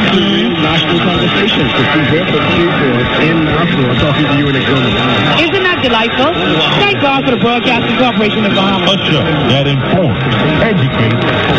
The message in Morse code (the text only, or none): none